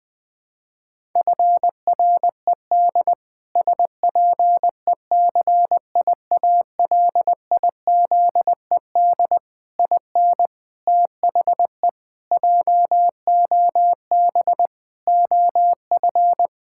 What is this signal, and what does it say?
Morse code, heard as a keyed tone, about 20 wpm, 700 Hz. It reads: FRED SPECIALIZED IN THE JOB OF
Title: Pangram Challenge v2 - 20wpm